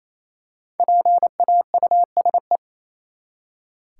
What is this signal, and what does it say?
Morse code, heard as a keyed tone, about 28 wpm, 700 Hz. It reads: PAUSE
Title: Five Letter Words - 28wpm